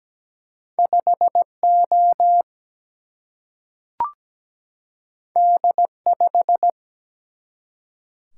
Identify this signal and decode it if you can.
Morse code, heard as a keyed tone, about 17 wpm, 700 Hz. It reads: 5 O D 5
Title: Single-Letter Single-Number 17wpm